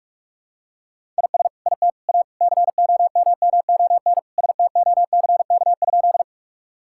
Morse code, heard as a keyed tone, about 45 wpm, 700 Hz. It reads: IS IT A COMMON STOCK?
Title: Sentences from Top 1000 Words - Review Part 5 - 45wpm